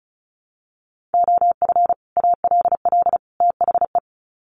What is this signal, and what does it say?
Morse code, heard as a keyed tone, about 34 wpm, 700 Hz. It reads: OF ALL THE
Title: Sets of 3 Words 35wpm